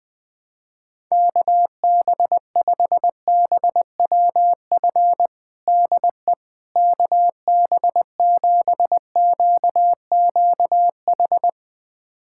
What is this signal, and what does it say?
Morse code, heard as a keyed tone, about 20 wpm, 700 Hz. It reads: KB5BWF DE KB7QQH
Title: USA Callsign DE Callsign 20wpm